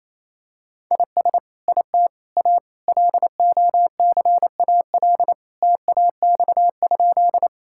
Morse code, heard as a keyed tone, about 28 wpm, 700 Hz. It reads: IS IT A LOCAL TAX?
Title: Sentences from Top 1000 Words - Review Part 5 - 28wpm